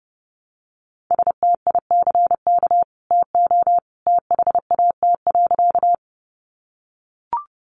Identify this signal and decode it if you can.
Morse code, heard as a keyed tone, about 30 wpm, 700 Hz. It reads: STICK TO THAT.
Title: Sentences from Top 1200 Words 30wpm